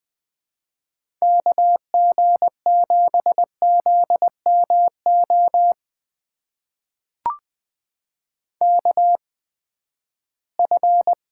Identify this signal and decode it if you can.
Morse code, heard as a keyed tone, about 20 wpm, spaced as for 14 wpm, 700 Hz. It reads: KG7ZMO KF
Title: USA Call Signs - 6 Characters - 8x Character Spacing - 20wpm